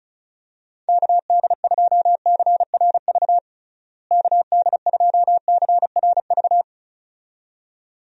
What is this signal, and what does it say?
Morse code, heard as a keyed tone, about 36 wpm, 700 Hz. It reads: KD2CRV KD2CRV
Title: USA Call Signs - 6 Characters - Repeated 2x - 35wpm